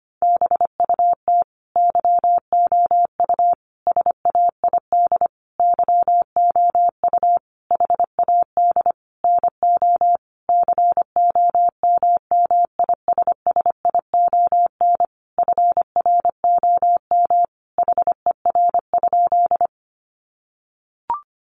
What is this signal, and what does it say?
Morse code, heard as a keyed tone, about 25 wpm, 700 Hz. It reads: BUT YOU SAID YOU HAD NO COMMISSION FROM HER?
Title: Sentences from Top 1400 Words 25wpm